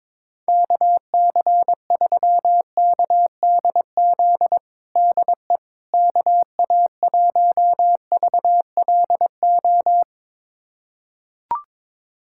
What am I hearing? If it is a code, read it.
Morse code, heard as a keyed tone, about 22 wpm, 700 Hz. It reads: KC3KDZ DE KA1VLO
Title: USA Callsign DE Callsign - 6 Characters - 22wpm